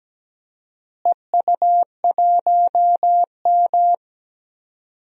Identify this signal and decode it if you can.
Morse code, heard as a keyed tone, about 17 wpm, 700 Hz. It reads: EU1M
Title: DXCC-100 Call Signs - 4 Characters - Repeated 2x - 17wpm